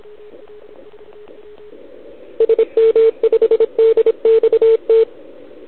Morse code, heard as a keyed tone, about 25 wpm, 450 Hz. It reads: SM5DXT
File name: ARCHIV / OKDX2020 / OL4Wa